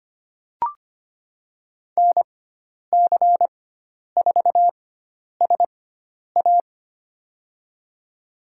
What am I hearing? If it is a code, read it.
Morse code, heard as a keyed tone, about 25 wpm, spaced as for 10 wpm, 700 Hz. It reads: NC4SA
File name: Lesson 16g - Letter c - Call Signs 25wpm